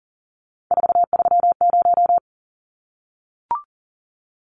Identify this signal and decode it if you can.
Morse code, heard as a keyed tone, about 39 wpm, 700 Hz. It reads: 430
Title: Numbers - 3-Digit 40wpm